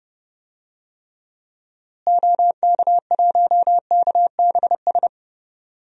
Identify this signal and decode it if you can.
Morse code, heard as a keyed tone, about 32 wpm, 700 Hz. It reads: OK1KBS
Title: DXCC-100 Call Signs - 6 Characters - Repeated 2x - 30wpm